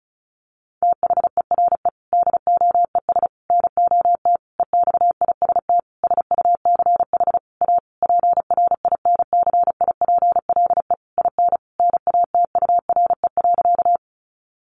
Morse code, heard as a keyed tone, about 34 wpm, 700 Hz. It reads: THERE DOES NOT EXIST SUCH A PRINCIPLE IN NATURE.
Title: Sentences from Top 1400 Words 35wpm